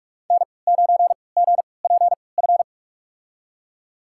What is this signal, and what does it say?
Morse code, heard as a keyed tone, about 42 wpm, spaced as for 25 wpm, 700 Hz. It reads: N9GPF
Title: USA Call Signs - 5 Characters - 3x Character Spacing - 45wpm